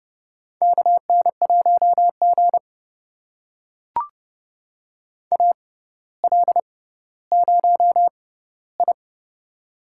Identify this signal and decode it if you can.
Morse code, heard as a keyed tone, about 30 wpm, spaced as for 15 wpm, 700 Hz. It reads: KN1G AL0I
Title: USA Call Signs - 4 Characters - 6x Character Spacing - 30wpm